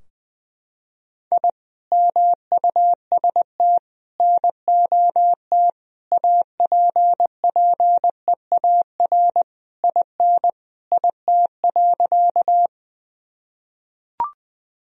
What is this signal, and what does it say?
Morse code, heard as a keyed tone, about 20 wpm, 700 Hz. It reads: I MUST NOT APPEAR IN IT.
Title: Sentences from Top 1000 Words - Review Part 2 - 20wpm